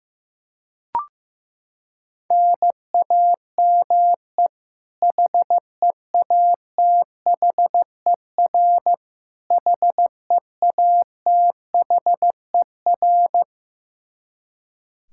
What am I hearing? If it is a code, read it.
Morse code, heard as a keyed tone, about 15 wpm, 700 Hz. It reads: NAME HEATHER HEATHER
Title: K1USN SST QSO Practice 15wpm